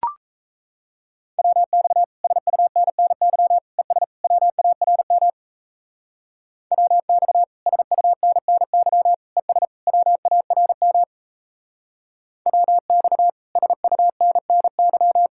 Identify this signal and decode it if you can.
Morse code, heard as a keyed tone, about 34 wpm, 700 Hz. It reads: WX SUNNY ES WARM WX SUNNY ES WARM WX SUNNY
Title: Speed-Racing Common QSO Phrases - 28wpm